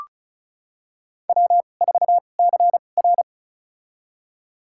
Morse code, heard as a keyed tone, about 36 wpm, spaced as for 25 wpm, 700 Hz. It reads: W4CR
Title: USA Call Signs - 2x Character Spacing - 35wpm